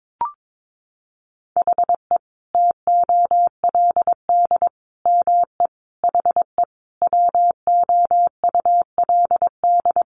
Morse code, heard as a keyed tone, about 22 wpm, 700 Hz. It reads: HE TOLD ME HE WOULD
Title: Sentences from Top 1300 Words 22wpm